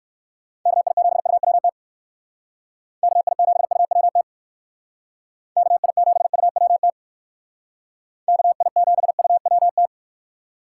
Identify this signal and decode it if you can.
Morse code, heard as a keyed tone, about 58 wpm, 700 Hz. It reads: KI7UWT KI7UWT KI7UWT KI7UWT